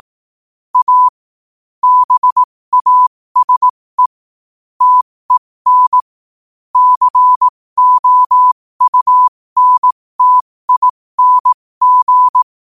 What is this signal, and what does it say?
Morse code, heard as a keyed tone, about 18 wpm, spaced as for 15 wpm, 1000 Hz. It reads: A BASE TEN COUNTING
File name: qotdfiles / 15wpm